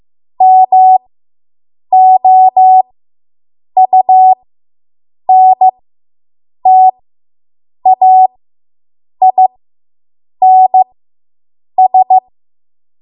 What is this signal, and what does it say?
Morse code, heard as a keyed tone, about 15 wpm, spaced as for 7 wpm, 750 Hz. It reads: MOUNTAINS